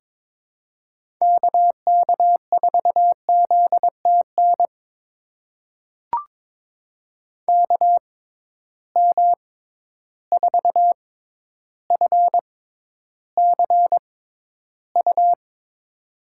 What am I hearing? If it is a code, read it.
Morse code, heard as a keyed tone, about 22 wpm, spaced as for 11 wpm, 700 Hz. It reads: KK4ZTN KM4FCU
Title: USA Call Signs - 6 Characters - 6x Character Spacing - 22wpm